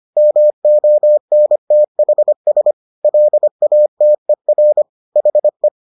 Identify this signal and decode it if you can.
Morse code, heard as a keyed tone, about 25 wpm, 600 Hz. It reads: MONTHS LATER HE